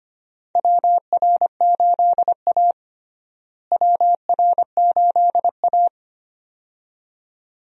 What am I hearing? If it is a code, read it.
Morse code, heard as a keyed tone, about 25 wpm, 700 Hz. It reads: WR8A WR8A
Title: USA Call Signs - 4 Characters - Repeated 2x - 25wpm